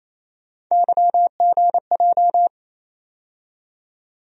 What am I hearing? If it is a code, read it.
Morse code, heard as a keyed tone, about 27 wpm, 700 Hz. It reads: YGJ
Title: USA Call Sign Postfixes - 3 Letters 28wpm